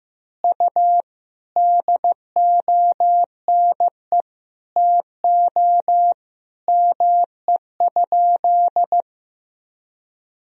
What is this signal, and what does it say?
Morse code, heard as a keyed tone, about 15 wpm, 700 Hz. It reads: U DONE TO ME?
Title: Sentences from Top 500 Words and Abbreviations 15wpm